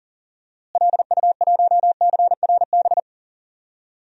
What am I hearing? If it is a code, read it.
Morse code, heard as a keyed tone, about 40 wpm, 700 Hz. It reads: LU1CRB
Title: DXCC-100 Call Signs - 6 Characters - Repeated 2x - 40wpm